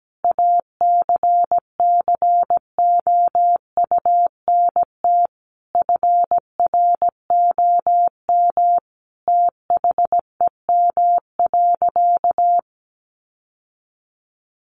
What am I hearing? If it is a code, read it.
Morse code, heard as a keyed tone, about 17 wpm, 700 Hz. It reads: ACCOUNT FROM THEM.
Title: Sentences from Top 900 Words 17wpm